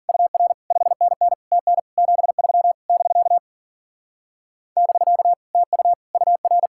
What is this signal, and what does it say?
Morse code, heard as a keyed tone, about 49 wpm, 700 Hz. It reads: UR 5NN TN 73 <BK> <BK> TU UR
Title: Speed-Racing Common QSO Phrases - 40wpm